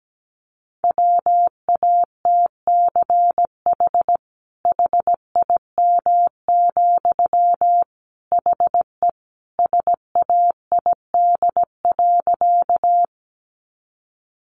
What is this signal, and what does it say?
Morse code, heard as a keyed tone, about 17 wpm, 700 Hz. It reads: WATCH HIM, HE SAID.
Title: Sentences from Top 300 Words 17wpm